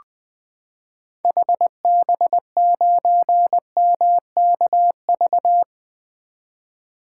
Morse code, heard as a keyed tone, about 20 wpm, 700 Hz. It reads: HB9MKV
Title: 2023 SOTA Chaser Call Signs - 20wpm